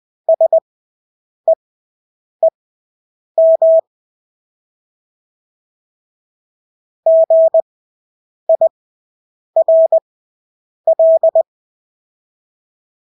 Morse code, heard as a keyed tone, about 20 wpm, spaced as for 8 wpm, 650 Hz. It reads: SEEM GIRL
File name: Engelske ord 20/8 wpm